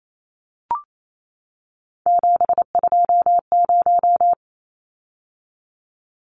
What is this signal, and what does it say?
Morse code, heard as a keyed tone, about 28 wpm, 700 Hz. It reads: 720
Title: Numbers - 3-Digit 28wpm